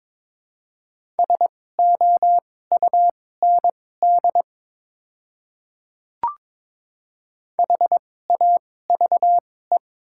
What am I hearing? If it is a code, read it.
Morse code, heard as a keyed tone, about 22 wpm, spaced as for 16 wpm, 700 Hz. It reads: SOUND HAVE
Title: 2x Character Spacing - Top 200 Words - 22wpm